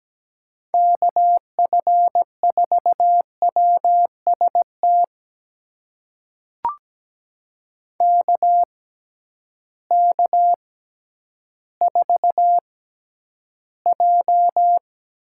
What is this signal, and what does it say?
Morse code, heard as a keyed tone, about 17 wpm, spaced as for 10 wpm, 700 Hz. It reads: KF4WST KK4J
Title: USA Call Signs - 6 Characters - 6x Character Spacing - 17wpm